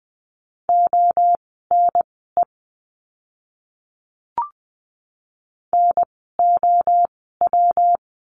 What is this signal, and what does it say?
Morse code, heard as a keyed tone, about 20 wpm, spaced as for 14 wpm, 700 Hz. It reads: ONE NOW
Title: 230 Lesson 37f - Letter x - Words 20wpm